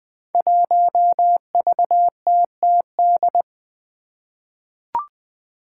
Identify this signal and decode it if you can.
Morse code, heard as a keyed tone, about 20 wpm, 700 Hz. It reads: 1VTTD
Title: Five Challenging Letter-Number Combinations 20wpm